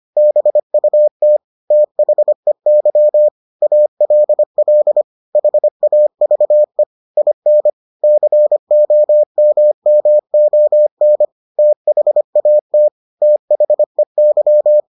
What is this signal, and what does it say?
Morse code, heard as a keyed tone, about 25 wpm, 600 Hz. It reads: BUT THEY ALL HAVE IN COMMON THAT THEY